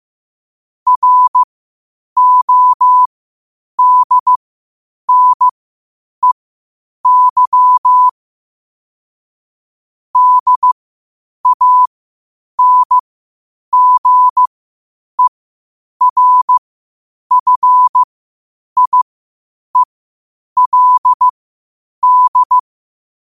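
Morse code, heard as a keyed tone, about 15 wpm, spaced as for 8 wpm, 1000 Hz. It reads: RODNEY DANGERFIELD
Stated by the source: Morse code quotes at 8 WPM.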